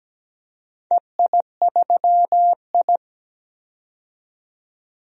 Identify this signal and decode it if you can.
Morse code, heard as a keyed tone, about 17 wpm, 700 Hz. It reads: EI3I
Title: DXCC-100 Call Signs - 4 Characters - 17wpm